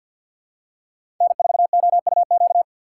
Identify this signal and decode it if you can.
Morse code, heard as a keyed tone, about 52 wpm, 700 Hz. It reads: N4OUQ
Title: USA Call Signs - 5 Characters - 4x Character Spacing - 50wpm